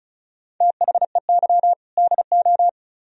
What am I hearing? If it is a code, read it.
Morse code, heard as a keyed tone, about 43 wpm, 700 Hz. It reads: THEY DO
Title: Speed-Racing - Sets of 2 Words 35wpm